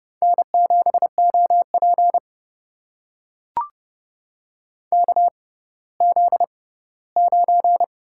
Morse code, heard as a keyed tone, about 31 wpm, spaced as for 17 wpm, 700 Hz. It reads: N7OP KZ9
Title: USA Call Signs - 4 Characters - 6x Character Spacing - 30wpm